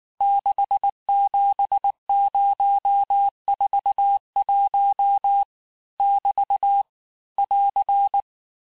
Morse code, heard as a keyed tone, about 19 wpm, 800 Hz. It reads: 67041 <BT> <AR>
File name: cours-du-lundi files / cours-1589-texte-chiffre-vitesse-18mpm